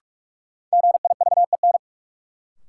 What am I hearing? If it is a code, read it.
Morse code, heard as a keyed tone, about 41 wpm, 700 Hz. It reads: GIVEN
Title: Sentences from Top 1400 Words - New Word Drill Alt - 45wpm